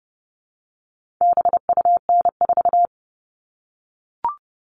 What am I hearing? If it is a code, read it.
Morse code, heard as a keyed tone, about 30 wpm, 700 Hz. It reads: BUN4
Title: All Cut Numbers - 4-Digit 30wpm